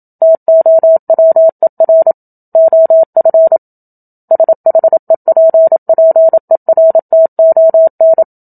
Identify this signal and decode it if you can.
Morse code, heard as a keyed tone, about 27 wpm, 650 Hz. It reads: TOWER OF SHEPPERTON
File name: Book Two / 01-Chapter 1 - 27 WPM